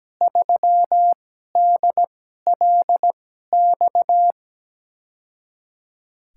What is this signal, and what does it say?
Morse code, heard as a keyed tone, about 17 wpm, spaced as for 12 wpm, 700 Hz. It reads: 3DLX
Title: USA Call Signs - 6 Characters - 2x Character Spacing - 17wpm